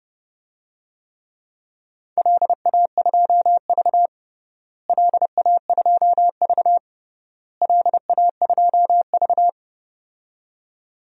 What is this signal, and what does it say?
Morse code, heard as a keyed tone, about 30 wpm, 700 Hz. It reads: LA2V LA2V LA2V
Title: DXCC-100 Call Signs - 4 Characters - Repeated 3x - 30wpm